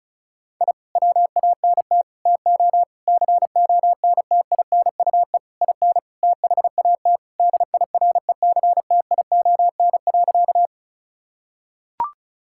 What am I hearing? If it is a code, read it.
Morse code, heard as a keyed tone, about 36 wpm, 700 Hz. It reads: I WANT TO CONTINUE IN THAT DIRECTION.
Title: Sentences from Top 1000 Words - Review Part 6 - 35wpm